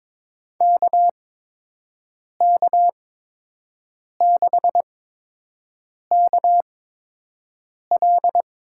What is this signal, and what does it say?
Morse code, heard as a keyed tone, about 22 wpm, spaced as for 6 wpm, 700 Hz. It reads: KK6KL
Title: USA Call Signs - 6 Characters - 8x Character Spacing - 22wpm